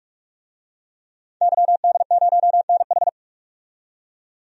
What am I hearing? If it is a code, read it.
Morse code, heard as a keyed tone, about 41 wpm, 700 Hz. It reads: YD0NH
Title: DXCC-100 Call Signs - 5 Characters - 45wpm